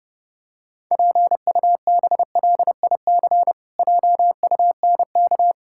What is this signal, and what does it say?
Morse code, heard as a keyed tone, about 30 wpm, 700 Hz. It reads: PUBLIC JUNK
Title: Pangram Challenge v2 - 30wpm